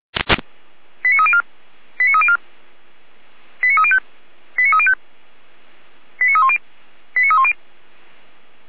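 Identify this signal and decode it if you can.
Five-tone selective call (ZVEI), heard as quick runs of five tones, tones of about 70 ms. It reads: 89304 89304 89305 89305 89310 89310
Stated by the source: ZVEI five-tone selective call received by SDR.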